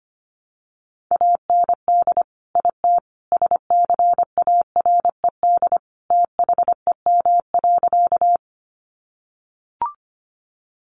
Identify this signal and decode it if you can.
Morse code, heard as a keyed tone, about 25 wpm, 700 Hz. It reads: AND IT SCARED THEM.
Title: Sentences from Top 700 Words 25wpm